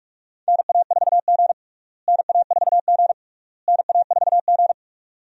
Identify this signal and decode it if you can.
Morse code, heard as a keyed tone, about 46 wpm, 700 Hz. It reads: NA4G NA4G NA4G
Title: USA Call Signs - 4 Characters - Repeated 3x - 45wpm